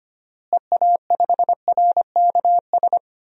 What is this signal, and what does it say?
Morse code, heard as a keyed tone, about 25 wpm, 700 Hz. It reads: EA5RKS
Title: DXCC-100 Call Signs - 6 Characters - Repeated 2x - 25wpm